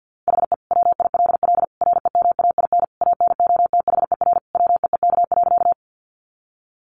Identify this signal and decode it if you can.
Morse code, heard as a keyed tone, about 50 wpm, 700 Hz. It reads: HE WILL REMAIN ANOTHER WEEK.
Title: Sentences from Top 1200 Words 50wpm